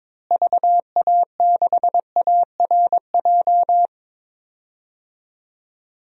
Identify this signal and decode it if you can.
Morse code, heard as a keyed tone, about 22 wpm, 700 Hz. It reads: VA6ARJ
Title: DXCC-100 Call Signs - 6 Characters - Repeated 2x - 22wpm